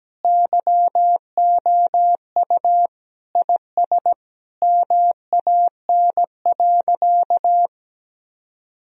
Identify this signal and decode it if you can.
Morse code, heard as a keyed tone, about 17 wpm, 700 Hz. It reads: YOU IS MAN.
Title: Sentences from Top 1000 Words - Review Part 3 - 17wpm